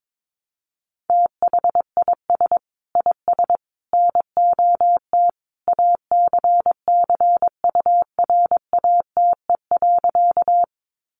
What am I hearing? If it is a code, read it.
Morse code, heard as a keyed tone, about 22 wpm, 700 Hz. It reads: THIS IS NOT ACCURATE.
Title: Sentences from Top 1200 Words 22wpm